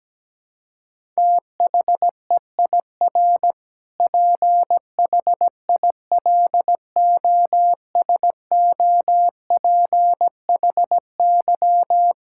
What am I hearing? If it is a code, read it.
Morse code, heard as a keyed tone, about 17 wpm, 700 Hz. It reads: THEIR PHILOSOPHY
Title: Sentences from Top 1000 Words - Review Part 6 - 17wpm